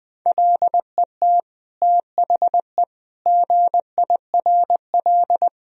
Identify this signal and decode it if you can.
Morse code, heard as a keyed tone, about 20 wpm, 700 Hz. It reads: LET THE GIRL
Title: Sentences from Top 1000 Words - Review Part 1 - 20wpm